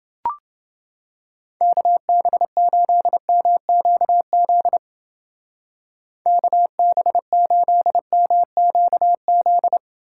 Morse code, heard as a keyed tone, about 29 wpm, 700 Hz. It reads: KB8MQZ KB8MQZ